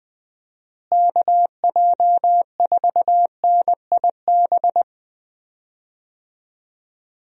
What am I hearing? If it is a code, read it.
Morse code, heard as a keyed tone, about 20 wpm, 700 Hz. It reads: KJ4NIB
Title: USA Call Signs - 6 Characters - Repeated 3x - 20wpm